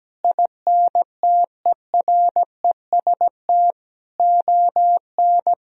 Morse code, heard as a keyed tone, about 17 wpm, 700 Hz. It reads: INTEREST ON
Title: Sentences from Top 1000 Words - Review Part 4 - 17wpm